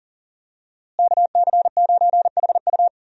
Morse code, heard as a keyed tone, about 40 wpm, 700 Hz. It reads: KC9HU
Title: USA Call Signs - 5 Characters - Repeated 3x - 40wpm